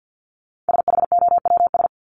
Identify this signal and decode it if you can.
Morse code, heard as a keyed tone, about 48 wpm, 700 Hz. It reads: SHOWS
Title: ES:S Suffix - Top 100 Words with Base Words 50wpm